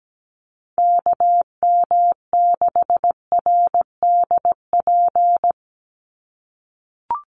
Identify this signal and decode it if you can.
Morse code, heard as a keyed tone, about 17 wpm, 700 Hz. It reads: KM6RDP
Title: USA Call Signs 17wpm